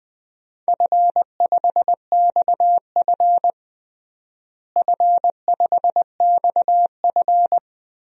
Morse code, heard as a keyed tone, about 20 wpm, 700 Hz. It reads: F5XF F5XF
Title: DXCC-100 Call Signs - 4 Characters - Repeated 3x - 20wpm